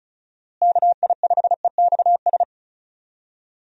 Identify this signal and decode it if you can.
Morse code, heard as a keyed tone, about 36 wpm, 700 Hz. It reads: KI5EXS
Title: USA Call Signs - 6 Characters - Repeated 3x - 35wpm